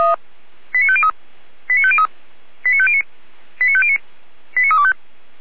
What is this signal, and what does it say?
Five-tone selective call (ZVEI), heard as quick runs of five tones, tones of about 70 ms. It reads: 89503 89503 89509 89509 89325